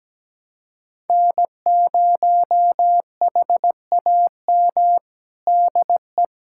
Morse code, heard as a keyed tone, about 17 wpm, 700 Hz. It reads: N0HAM DE
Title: USA Callsign DE Callsign - 5 Characters - 17wpm